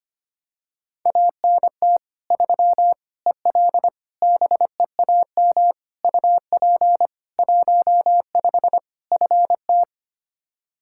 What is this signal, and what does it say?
Morse code, heard as a keyed tone, about 25 wpm, 700 Hz. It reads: ANT 3 EL BEAM UP 15 FT
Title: Practice QSO - Exchange 2 - Info - 25wpm